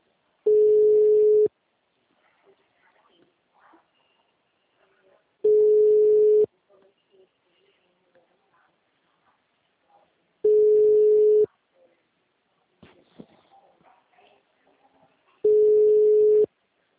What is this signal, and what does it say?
Ringback tone (Europe, 425 Hz).